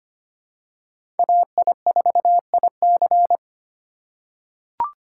Morse code, heard as a keyed tone, about 25 wpm, 700 Hz. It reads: AI4IC
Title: USA Call Signs - 5 Characters - 4x Character Spacing - 25wpm